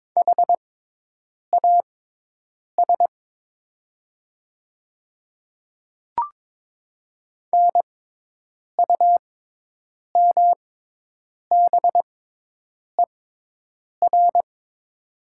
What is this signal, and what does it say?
Morse code, heard as a keyed tone, about 22 wpm, spaced as for 8 wpm, 700 Hz. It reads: HAS NUMBER
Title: 6x Character Spacing - Top 100 Words - 22wpm